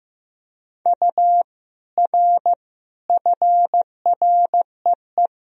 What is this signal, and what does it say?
Morse code, heard as a keyed tone, about 15 wpm, 700 Hz. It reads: U R FREE
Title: Sentences from Top 500 Words and Abbreviations 15wpm